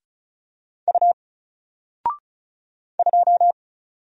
Morse code, heard as a keyed tone, about 35 wpm, 700 Hz. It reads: U 2
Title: Single Letter-Number Alt 35wpm